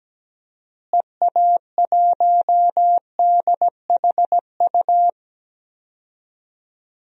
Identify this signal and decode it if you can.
Morse code, heard as a keyed tone, about 17 wpm, 700 Hz. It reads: EA1DHU
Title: DXCC-100 Call Signs - 6 Characters - Repeated 3x - 17wpm